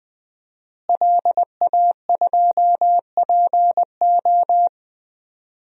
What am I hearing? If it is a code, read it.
Morse code, heard as a keyed tone, about 21 wpm, 700 Hz. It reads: LA2PO
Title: DXCC-100 Call Signs - 5 Characters - Repeated 3x - 20wpm